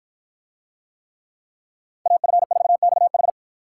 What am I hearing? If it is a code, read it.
Morse code, heard as a keyed tone, about 53 wpm, 700 Hz. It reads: AF4XH